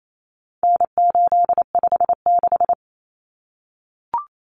Morse code, heard as a keyed tone, about 28 wpm, 700 Hz. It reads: N856
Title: Common Cut Numbers - 4-Digit 28wpm